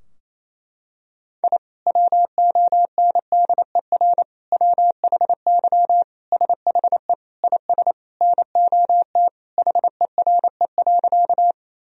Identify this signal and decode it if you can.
Morse code, heard as a keyed tone, about 28 wpm, 700 Hz. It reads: I WONDER WHY SHE IS NOT HERE.
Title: Sentences from Top 1000 Words - Review Part 3 - 28wpm